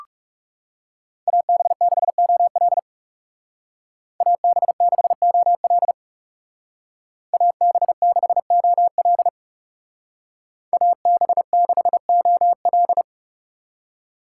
Morse code, heard as a keyed tone, about 36 wpm, 700 Hz. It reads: AB6OL AB6OL AB6OL AB6OL
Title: Speed Racing - USA 2x2 Call Signs 30wpm